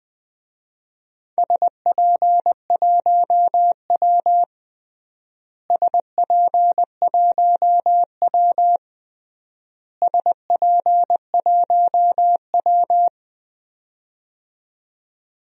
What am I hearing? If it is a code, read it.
Morse code, heard as a keyed tone, about 20 wpm, 700 Hz. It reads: SP1W SP1W SP1W
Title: DXCC-100 Call Signs - 4 Characters - Repeated 3x - 20wpm